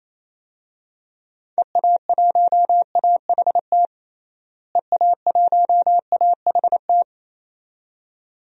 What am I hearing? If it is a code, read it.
Morse code, heard as a keyed tone, about 27 wpm, 700 Hz. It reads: EA1AHT EA1AHT
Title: DXCC-100 Call Signs - 6 Characters - Repeated 2x - 28wpm